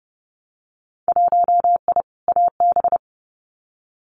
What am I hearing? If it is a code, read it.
Morse code, heard as a keyed tone, about 30 wpm, 700 Hz. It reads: 1I AB
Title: Winter Field Day Exchanges Encore 30wpm